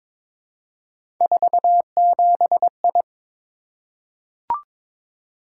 Morse code, heard as a keyed tone, about 22 wpm, 700 Hz. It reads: 47I
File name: practice / 22wpm / Three Letter-Number Combinations - 22wpm